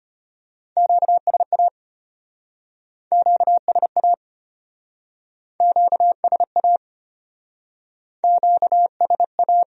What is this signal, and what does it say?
Morse code, heard as a keyed tone, about 33 wpm, 700 Hz. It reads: QSA QSA QSA QSA